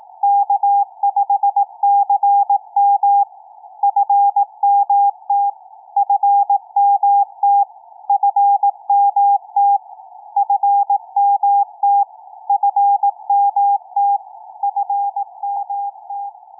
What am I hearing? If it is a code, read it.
Morse code, heard as a keyed tone, about 18 wpm, 800 Hz. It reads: K5CM FMT FMT FMT FMT FMT FMT
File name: k5cm-20m-9-11-by-ve2iq